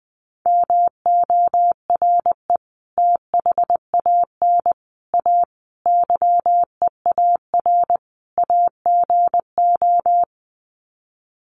Morse code, heard as a keyed tone, about 19 wpm, 700 Hz. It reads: MORE THAN A YEAR AGO
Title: Sets of 5 Words Encore2 20wpm